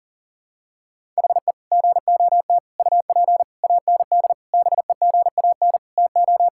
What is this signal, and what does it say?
Morse code, heard as a keyed tone, about 40 wpm, 700 Hz. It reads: HE GOT UP AND BEGAN TO
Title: Sentences from Top 1600 Words 40wpm